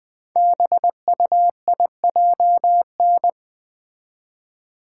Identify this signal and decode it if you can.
Morse code, heard as a keyed tone, about 20 wpm, 700 Hz. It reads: BUIJN
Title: Code Groups - 5 Letters - 20wpm